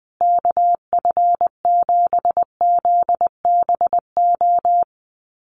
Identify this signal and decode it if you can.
Morse code, heard as a keyed tone, about 19 wpm, 700 Hz. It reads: KF7ZBO
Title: USA Call Signs Repeated 3x 20wpm